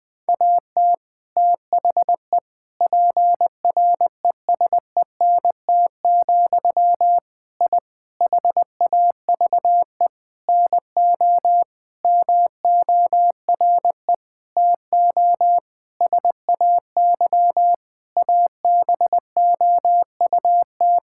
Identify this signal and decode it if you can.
Morse code, heard as a keyed tone, about 20 wpm, 700 Hz. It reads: AT THE PRESENT, I HAVE NO MORE TO SAY ABOUT
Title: Sentences from Top 1000 Words - Review Part 5 - 20wpm